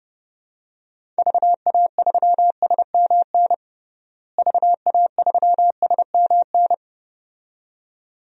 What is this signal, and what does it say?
Morse code, heard as a keyed tone, about 31 wpm, 700 Hz. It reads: VA3SMN VA3SMN
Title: DXCC-100 Call Signs - 6 Characters - Repeated 2x - 30wpm